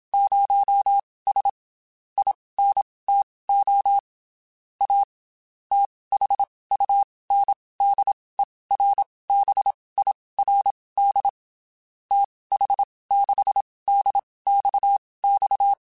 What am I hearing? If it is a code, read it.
Morse code, heard as a keyed tone, about 27 wpm, spaced as for 19 wpm, 800 Hz. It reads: 0S INTO A THUNDERBIRD TH6DXX
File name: QSOs / QSOs-mp3s / WD0FBT 1@25